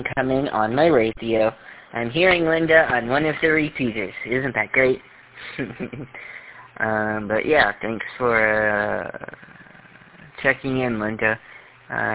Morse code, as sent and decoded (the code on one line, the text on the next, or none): none